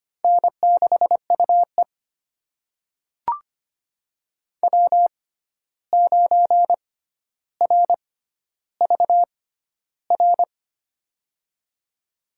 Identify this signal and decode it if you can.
Morse code, heard as a keyed tone, about 25 wpm, spaced as for 12 wpm, 700 Hz. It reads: N6UE W9RVR
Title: USA Call Signs - 6x Character Spacing - 25wpm